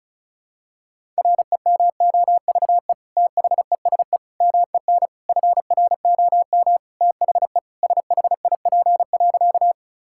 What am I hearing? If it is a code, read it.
Morse code, heard as a keyed tone, about 34 wpm, 700 Hz. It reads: REMOVE THESE MEN FROM THE SHIP.
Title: Sentences from Top 1000 Words - Review Part 5 - 35wpm